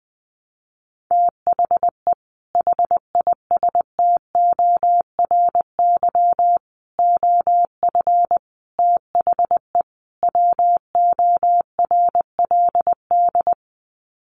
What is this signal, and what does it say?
Morse code, heard as a keyed tone, about 20 wpm, 700 Hz. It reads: THE HISTORY OF THE WORLD
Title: Sets of 5 Words Encore2 20wpm